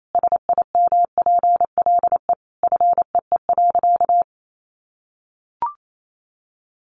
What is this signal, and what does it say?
Morse code, heard as a keyed tone, about 28 wpm, 700 Hz. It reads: SIMPLE FEE.
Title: Sentences from Top 1500 Words 28wpm